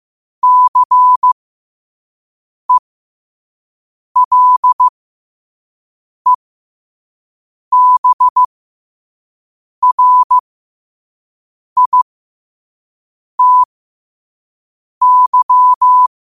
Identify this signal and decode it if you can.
Morse code, heard as a keyed tone, about 15 wpm, spaced as for 5 wpm, 1000 Hz. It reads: CELEBRITY
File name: 5wpm